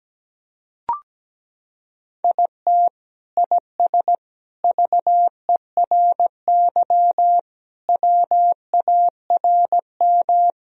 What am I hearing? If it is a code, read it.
Morse code, heard as a keyed tone, about 17 wpm, 700 Hz. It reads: IT IS VERY WARM
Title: Sentences from Top 1000 Words - Review Part 2 - 17wpm